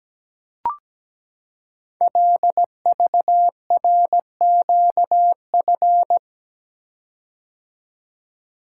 Morse code, heard as a keyed tone, about 17 wpm, 700 Hz. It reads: LVRQF
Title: Code Groups - 5 Letters - 17wpm